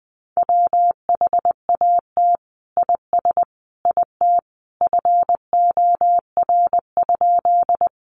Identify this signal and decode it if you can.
Morse code, heard as a keyed tone, about 20 wpm, 700 Hz. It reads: WHAT IS IT FOR?
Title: Sentences from Top 100 Words 20wpm